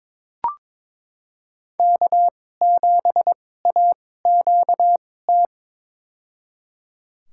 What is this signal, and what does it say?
Morse code, heard as a keyed tone, about 22 wpm, spaced as for 16 wpm, 700 Hz. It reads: K7AQT
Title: USA Call Signs - 5 Characters - 2x Character Spacing - 22wpm